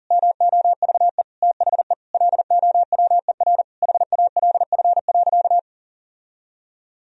Morse code, heard as a keyed tone, about 40 wpm, 700 Hz. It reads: MOVE THE LOWER HALF.
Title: Sentences from Top 1000 Words - Review Part 4 - 40wpm